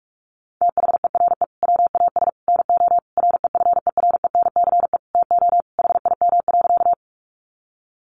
Morse code, heard as a keyed tone, about 44 wpm, 700 Hz. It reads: THERE WAS NO REFERENCE TO HIM.
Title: Sentences from Top 1200 Words 45wpm